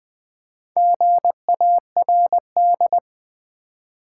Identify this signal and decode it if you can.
Morse code, heard as a keyed tone, about 20 wpm, 700 Hz. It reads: GARD
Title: Build A Word - 2 Letters At A Time - 6-Letter Words - 20wpm